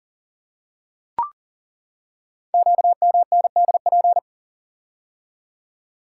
Morse code, heard as a keyed tone, about 42 wpm, 690 Hz. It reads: QMNDP
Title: Code Groups - 5 Letters - 40wpm